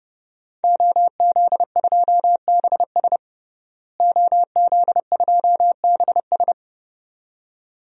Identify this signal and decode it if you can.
Morse code, heard as a keyed tone, about 30 wpm, 700 Hz. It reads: OZ2BS OZ2BS
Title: DXCC-100 Call Signs - 5 Characters - Repeated 2x - 30wpm